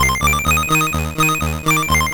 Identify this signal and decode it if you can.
There is no code.